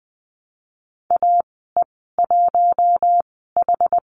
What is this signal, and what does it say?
Morse code, heard as a keyed tone, about 20 wpm, spaced as for 14 wpm, 700 Hz. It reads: AE1H
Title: 082 Lesson 14g - Number 2 - Call Signs 20wpm